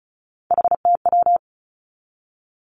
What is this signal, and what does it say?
Morse code, heard as a keyed tone, about 36 wpm, 700 Hz. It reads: HTW
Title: Three Challenging Letter-Number Combinations 35wpm